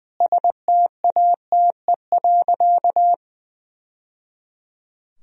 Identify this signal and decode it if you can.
Morse code, heard as a keyed tone, about 21 wpm, 700 Hz. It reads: STATE.
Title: Sentences from Top 1000 Words - Review Part 2 - 20wpm